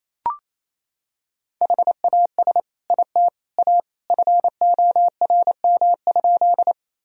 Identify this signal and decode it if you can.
Morse code, heard as a keyed tone, about 28 wpm, 700 Hz. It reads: HAS IT A FORM?
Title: Sentences from Top 1000 Words - Review Part 1 - 28wpm